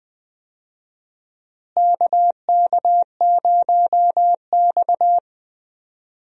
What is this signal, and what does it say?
Morse code, heard as a keyed tone, about 20 wpm, 700 Hz. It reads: KK0X
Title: USA Call Signs - 4 Characters - Repeated 3x - 20wpm